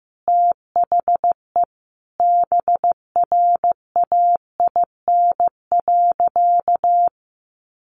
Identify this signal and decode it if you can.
Morse code, heard as a keyed tone, about 15 wpm, 700 Hz. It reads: THE BRAIN.
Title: Sentences from Top 1100 Words 15wpm